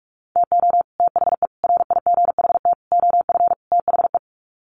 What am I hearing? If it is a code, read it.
Morse code, heard as a keyed tone, about 44 wpm, 700 Hz. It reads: TO THE RIGHT OF THE
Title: Sets of 5 Words Encore 45wpm